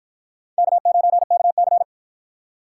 Morse code, heard as a keyed tone, about 49 wpm, 700 Hz. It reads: K9KC